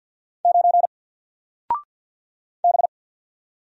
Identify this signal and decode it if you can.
Morse code, heard as a keyed tone, about 48 wpm, 700 Hz. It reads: 9 B
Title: Single Character (Letter:Number:Punctuation:Prosign) Alt 50wpm